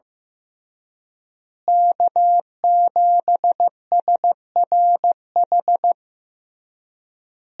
Morse code, heard as a keyed tone, about 15 wpm, 700 Hz. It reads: K7SRH
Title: USA Call Signs - 5 Characters - Repeated 3x - 15wpm